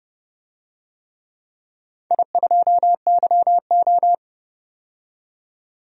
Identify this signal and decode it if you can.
Morse code, heard as a keyed tone, about 30 wpm, 700 Hz. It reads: I2YO
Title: DXCC-100 Call Signs - 4 Characters - 30wpm